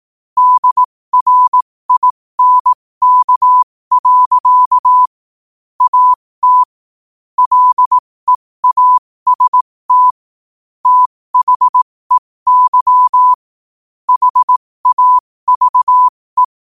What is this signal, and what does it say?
Morse code, heard as a keyed tone, about 18 wpm, spaced as for 15 wpm, 1000 Hz. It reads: DRINK. AT LEAST THEY HAVE
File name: qotdfiles / 15wpm